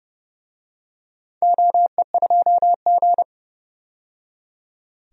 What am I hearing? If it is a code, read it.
Morse code, heard as a keyed tone, about 31 wpm, 700 Hz. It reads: OE2G
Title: DXCC-100 Call Signs - 4 Characters - 30wpm